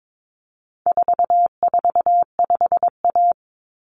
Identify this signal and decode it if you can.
Morse code, heard as a keyed tone, about 22 wpm, 700 Hz. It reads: 445A
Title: Common Cut Numbers - 4-Digit 22wpm